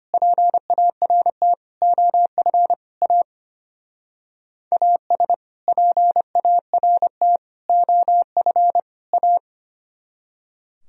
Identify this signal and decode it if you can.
Morse code, heard as a keyed tone, about 31 wpm, 700 Hz. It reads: PART OF A AS PART OF A